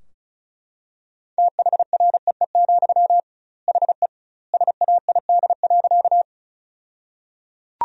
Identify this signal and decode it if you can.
Morse code, heard as a keyed tone, about 36 wpm, 700 Hz. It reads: THREE, HE SAID.
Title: 2x Word-Spacing - Sentences from Top 200 Words - 35wpm